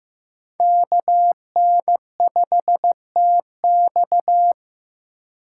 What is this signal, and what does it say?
Morse code, heard as a keyed tone, about 15 wpm, 700 Hz. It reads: KN5TX
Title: K1USN SST QSO Practice - 15wpm